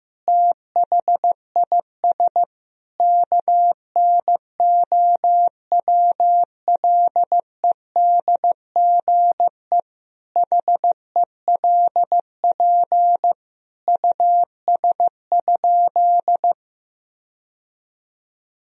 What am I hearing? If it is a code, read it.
Morse code, heard as a keyed tone, about 15 wpm, 700 Hz. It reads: THIS KNOWLEDGE HELP US?
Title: Sentences from Top 1000 Words - Review Part 3 - 15wpm